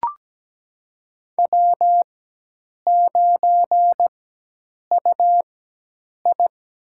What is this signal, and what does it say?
Morse code, heard as a keyed tone, about 17 wpm, spaced as for 8 wpm, 700 Hz. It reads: W9UI
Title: USA Call Signs - 4x Character Spacing - 17wpm